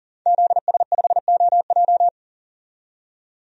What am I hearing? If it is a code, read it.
Morse code, heard as a keyed tone, about 40 wpm, 700 Hz. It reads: ZS5OJ
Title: DXCC-100 Call Signs - 5 Characters - 40wpm